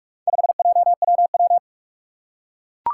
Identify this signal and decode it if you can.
Morse code, heard as a keyed tone, about 44 wpm, 700 Hz. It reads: 5JWW
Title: Four Challenging Letter-Number Combinations 45wpm